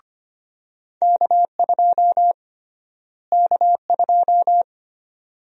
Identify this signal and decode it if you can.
Morse code, heard as a keyed tone, about 25 wpm, 700 Hz. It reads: K2 K2
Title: USA Call Sign Prefixes - Repeated 2x - 25wpm